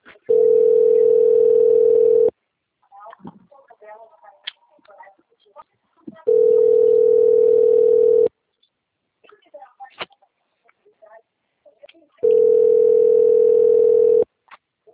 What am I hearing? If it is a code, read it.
Ringback tone (North America), 440 + 480 Hz, with speech.
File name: out- - - - - .40040